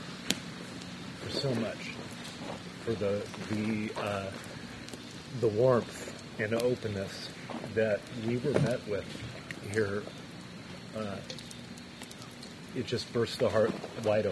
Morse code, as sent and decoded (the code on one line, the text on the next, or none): none